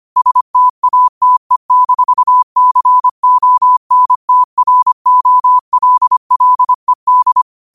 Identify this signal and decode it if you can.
Morse code, heard as a keyed tone, about 25 wpm, 1000 Hz. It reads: STATE-CONTROLLED
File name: qotdfiles / 25wpm